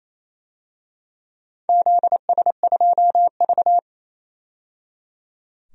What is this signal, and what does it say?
Morse code, heard as a keyed tone, about 28 wpm, 700 Hz. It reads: ZS2V
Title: DXCC-100 Call Signs - 4 Characters - 28wpm